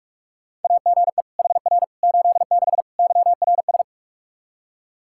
Morse code, heard as a keyed tone, about 46 wpm, 700 Hz. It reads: AGE HR 86 YRS
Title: Practice QSO - Exchange 3 - More Info - 45wpm